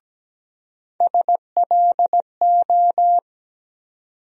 Morse code, heard as a keyed tone, about 17 wpm, 700 Hz. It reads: SLO
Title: USA Call Sign Postfixes - 3 Letters 17wpm